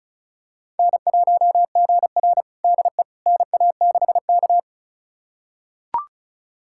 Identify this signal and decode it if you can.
Morse code, heard as a keyed tone, about 36 wpm, 700 Hz. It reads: N1GR DE NA6K
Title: USA Callsign DE Callsign - 4 Characters - 35wpm